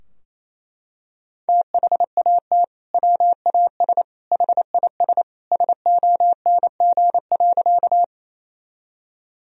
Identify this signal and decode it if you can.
Morse code, heard as a keyed tone, about 28 wpm, 700 Hz. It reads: THAT WAS HIS SONG.
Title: Sentences from Top 1000 Words - Review Part 2 - 28wpm